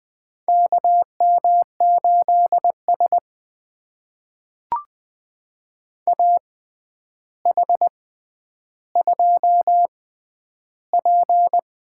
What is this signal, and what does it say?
Morse code, heard as a keyed tone, about 20 wpm, spaced as for 10 wpm, 700 Hz. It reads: KM8S AH2P
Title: USA Call Signs - 4 Characters - 6x Character Spacing - 20wpm